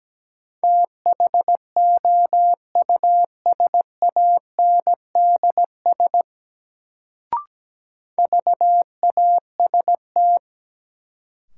Morse code, heard as a keyed tone, about 17 wpm, 700 Hz. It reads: THOUSANDS VAST
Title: Sets of 4 Words - New Word Drill - 17wpm